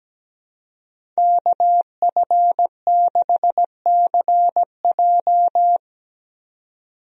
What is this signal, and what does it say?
Morse code, heard as a keyed tone, about 17 wpm, 700 Hz. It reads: KF6CJ
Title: USA Call Signs - 5 Characters - 17wpm